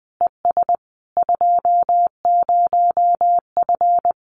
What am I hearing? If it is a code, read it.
Morse code, heard as a keyed tone, about 20 wpm, 700 Hz. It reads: ES 20F
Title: Common QSO Phrases Encore 20wpm